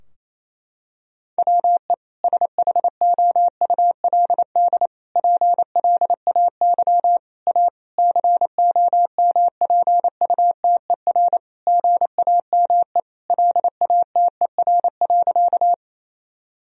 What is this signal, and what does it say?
Morse code, heard as a keyed tone, about 28 wpm, 700 Hz. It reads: WE SHOULD PLAY A COMPUTER GAME LATER.
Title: Sentences from Top 1000 Words - Review Part 3 - 28wpm